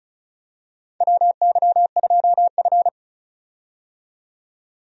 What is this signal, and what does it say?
Morse code, heard as a keyed tone, about 36 wpm, 700 Hz. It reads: WY2F
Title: USA Call Signs - 4 Characters - 35wpm